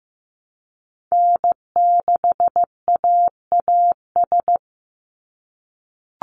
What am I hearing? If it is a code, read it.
Morse code, heard as a keyed tone, about 15 wpm, 700 Hz. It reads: N6AAS
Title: USA Call Signs 15wpm